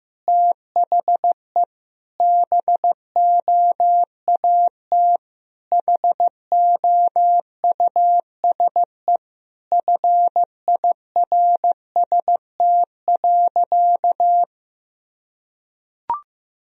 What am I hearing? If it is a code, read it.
Morse code, heard as a keyed tone, about 15 wpm, 700 Hz. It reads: THE BOAT HOUSE FIRST.
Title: Sentences from Top 1000 Words - Review Part 3 - 15wpm